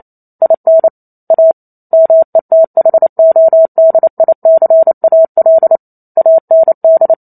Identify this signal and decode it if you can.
Morse code, heard as a keyed tone, about 28 wpm, 650 Hz. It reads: IN A METHODICAL AND